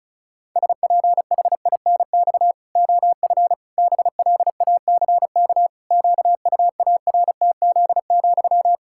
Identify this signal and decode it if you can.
Morse code, heard as a keyed tone, about 34 wpm, 700 Hz. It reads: SPHINX OF BLACK QUARTZ,
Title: Pangram Challenge v2 - 35wpm